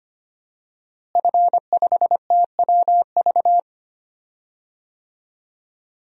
Morse code, heard as a keyed tone, about 25 wpm, 700 Hz. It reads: F5TWV
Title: DXCC-100 Call Signs - 5 Characters - Repeated 3x - 25wpm